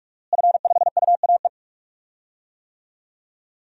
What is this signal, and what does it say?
Morse code, heard as a keyed tone, about 41 wpm, 700 Hz. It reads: F5UAE
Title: DXCC-100 Call Signs - 5 Characters - Repeated 3x - 45wpm